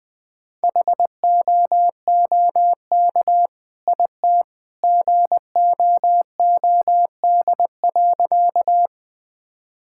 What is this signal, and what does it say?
Morse code, heard as a keyed tone, about 21 wpm, 700 Hz. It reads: HOOK IT GOOD.
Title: Sentences from Top 1600 Words 20wpm